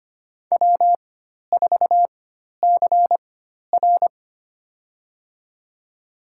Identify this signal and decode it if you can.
Morse code, heard as a keyed tone, about 25 wpm, spaced as for 12 wpm, 700 Hz. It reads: W4CR
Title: USA Call Signs - 4 Characters - 4x Character Spacing - 25wpm